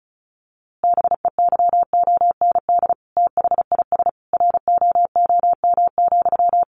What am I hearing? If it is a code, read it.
Morse code, heard as a keyed tone, about 34 wpm, 700 Hz. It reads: BEYOND THIS ROOM,
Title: A Sound Of Thunder 35wpm